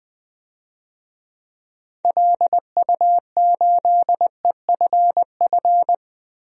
Morse code, heard as a keyed tone, about 21 wpm, 700 Hz. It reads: LU8EFF
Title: DXCC-100 Call Signs - 6 Characters - 20wpm